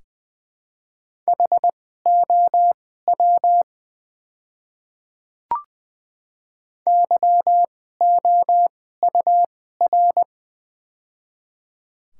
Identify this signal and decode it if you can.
Morse code, heard as a keyed tone, about 21 wpm, spaced as for 15 wpm, 700 Hz. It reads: HOW YOUR
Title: 2x Character Spacing - Top 200 Words - 20wpm